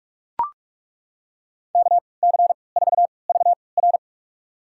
Morse code, heard as a keyed tone, about 43 wpm, spaced as for 25 wpm, 700 Hz. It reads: KC4VR
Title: USA Call Signs - 5 Characters - 3x Character Spacing - 45wpm